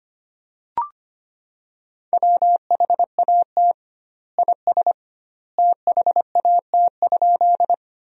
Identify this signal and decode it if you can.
Morse code, heard as a keyed tone, about 25 wpm, 700 Hz. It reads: WHAT IS THAT?
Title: 2x Word-Spacing - Sentences from Top 100 Words - 25wpm